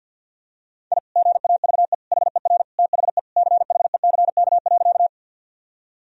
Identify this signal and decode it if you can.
Morse code, heard as a keyed tone, about 46 wpm, 700 Hz. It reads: I GAVE HER THE CHECK.
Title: Sentences from Top 1000 Words - Review Part 3 - 50wpm